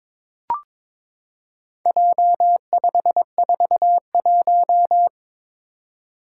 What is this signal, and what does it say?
Morse code, heard as a keyed tone, about 22 wpm, 700 Hz. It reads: J541
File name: practice / 22wpm / Four Letter-Number Combinations - 22wpm